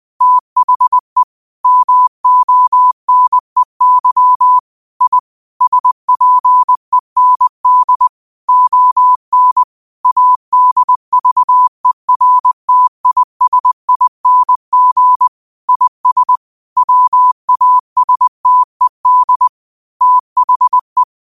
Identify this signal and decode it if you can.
Morse code, heard as a keyed tone, about 20 wpm, 1000 Hz. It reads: THE MONEY I SPEND ON ADVERTISING IS WASTED THE
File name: qotdfiles / 20wpm